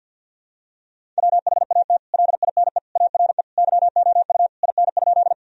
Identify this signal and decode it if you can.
Morse code, heard as a keyed tone, about 51 wpm, 700 Hz. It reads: WHAT LINE ARE YOU IN?
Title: Sentences from Top 1000 Words - Review Part 1 - 50wpm